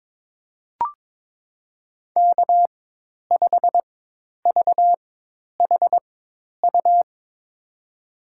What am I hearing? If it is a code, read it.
Morse code, heard as a keyed tone, about 22 wpm, spaced as for 10 wpm, 700 Hz. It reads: K5VHU
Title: USA Call Signs - 5 Characters - 4x Character Spacing - 22wpm